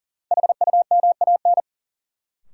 Morse code, heard as a keyed tone, about 40 wpm, 700 Hz. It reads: HUMAN
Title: Sentences from Top 600 Words - New Word Drill - 40wpm